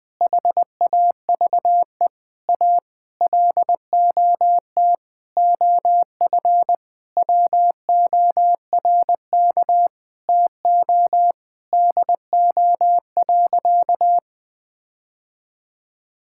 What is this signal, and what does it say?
Morse code, heard as a keyed tone, about 20 wpm, 700 Hz. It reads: HAVE A LOT OF WORK TO DO.
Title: Sentences with 8-Word Phrases - 20wpm